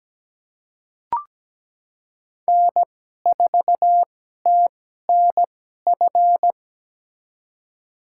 Morse code, heard as a keyed tone, about 17 wpm, spaced as for 12 wpm, 700 Hz. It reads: N4TNF
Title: USA Call Signs - 5 Characters - 2x Character Spacing - 17wpm